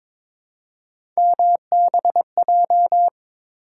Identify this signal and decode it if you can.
Morse code, heard as a keyed tone, about 21 wpm, 700 Hz. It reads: MBJ
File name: practice / 22wpm / Three Letter-Number Combinations - 22wpm